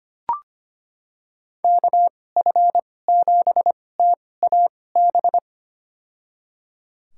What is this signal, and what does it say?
Morse code, heard as a keyed tone, about 25 wpm, spaced as for 18 wpm, 700 Hz. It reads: KF7TAB
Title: USA Call Signs - 6 Characters - 2x Character Spacing - 25wpm